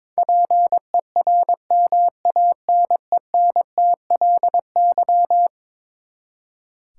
Morse code, heard as a keyed tone, about 22 wpm, 700 Hz. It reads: PERMANENTLY
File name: practice / 22wpm / Eleven Letter Words - 22wpm